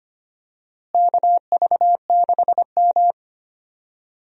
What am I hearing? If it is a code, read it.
Morse code, heard as a keyed tone, about 25 wpm, 700 Hz. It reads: KV6M
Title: USA Call Signs - 4 Characters - Repeated 3x - 25wpm